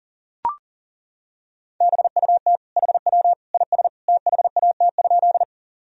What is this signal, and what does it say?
Morse code, heard as a keyed tone, about 40 wpm, 700 Hz. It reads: BUT HW IS THAT?
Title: Sentences from Top 100 Words and Abbreviations 40wpm